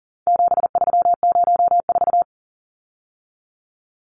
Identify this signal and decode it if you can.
Morse code, heard as a keyed tone, about 40 wpm, 700 Hz. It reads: 7304
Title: Numbers - 4-Digit 40wpm